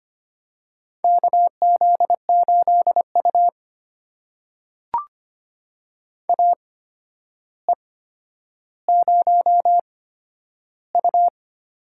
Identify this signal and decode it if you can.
Morse code, heard as a keyed tone, about 25 wpm, spaced as for 11 wpm, 700 Hz. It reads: KZ8U AE0U
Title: USA Call Signs - 4 Characters - 8x Character Spacing - 25wpm